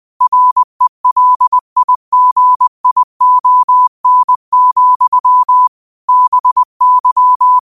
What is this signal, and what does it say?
Morse code, heard as a keyed tone, about 20 wpm, 1000 Hz. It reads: RELIGION, BY
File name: qotdfiles / 20wpm